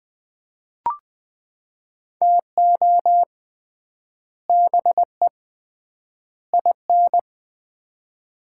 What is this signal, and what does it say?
Morse code, heard as a keyed tone, about 20 wpm, 700 Hz. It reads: TO BE IN
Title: 3x Word-Spacing - Sets of 3 Words - 20wpm